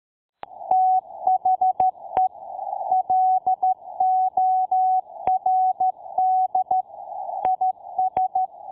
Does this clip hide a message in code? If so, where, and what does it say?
Yes: Morse code from 0.7 to 8.4 s, heard as a keyed tone, about 13 wpm, 750 Hz. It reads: THE LORD IS